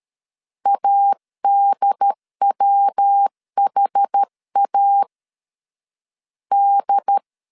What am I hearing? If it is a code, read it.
Morse code, heard as a keyed tone, about 13 wpm, 800 Hz. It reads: ADWHA D
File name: CW / DET / 1 3 1500C S A 12wpm 2905